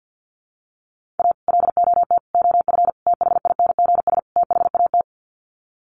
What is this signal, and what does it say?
Morse code, heard as a keyed tone, about 50 wpm, 700 Hz. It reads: A LOT OF THINGS THAT
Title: Sets of 5 Words 50wpm